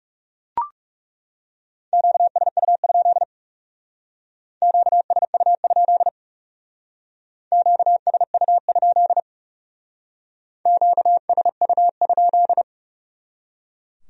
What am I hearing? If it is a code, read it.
Morse code, heard as a keyed tone, about 37 wpm, 700 Hz. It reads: QSU? QSU? QSU? QSU?
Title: Speed-Racing Q-Codes - 30wpm